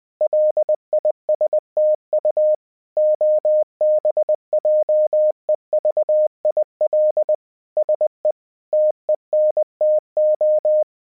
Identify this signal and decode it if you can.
Morse code, heard as a keyed tone, about 20 wpm, 610 Hz. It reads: LISTU OBJEVIL SE TENTO
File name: 20wpm